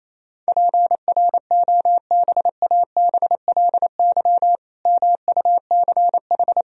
Morse code, heard as a keyed tone, about 28 wpm, 700 Hz. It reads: PROBABLY MUCH
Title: Sentences from Top 1000 Words - Review Part 4 - 28wpm